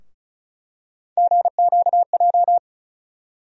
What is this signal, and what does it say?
Morse code, heard as a keyed tone, about 36 wpm, 700 Hz. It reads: GQJ
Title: USA Call Sign Postfixes - 3 Letters 35wpm